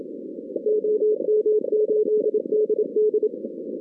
Morse code, heard as a keyed tone, about 27 wpm, 430 Hz. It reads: OM8DD
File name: 2016 arrl dx cw / 2016-ARRL-DX-CW-K5TR- - 0-41